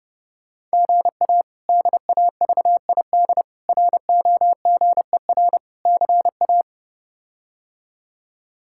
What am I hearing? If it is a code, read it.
Morse code, heard as a keyed tone, about 30 wpm, 700 Hz. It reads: GA DAVID ROGER CA
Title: K1USN SST QSO Practice - 30wpm